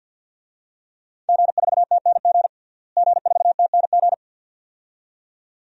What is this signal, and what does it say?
Morse code, heard as a keyed tone, about 51 wpm, 700 Hz. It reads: G4TNG G4TNG
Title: DXCC-100 Call Signs - 5 Characters - Repeated 2x - 50wpm